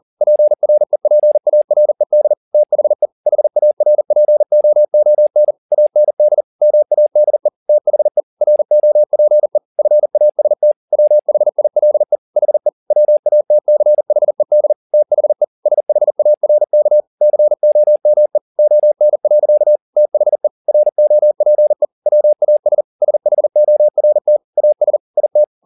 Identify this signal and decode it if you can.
Morse code, heard as a keyed tone, about 40 wpm, 600 Hz. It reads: PREPARED THE HARPOON AND MADE THE ROPE FAST WHILE HE WATCHED THE SHARK COME ON. THE ROPE WAS SHORT AS IT
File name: the old man and the sea 2 40wpm